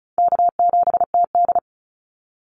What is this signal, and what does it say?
Morse code, heard as a keyed tone, about 34 wpm, 700 Hz. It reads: K7TD
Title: Call Signs - 35wpm